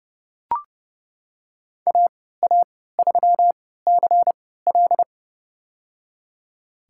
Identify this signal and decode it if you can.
Morse code, heard as a keyed tone, about 31 wpm, spaced as for 17 wpm, 700 Hz. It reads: AA3CL
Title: USA Call Signs - 5 Characters - 3x Character Spacing - 30wpm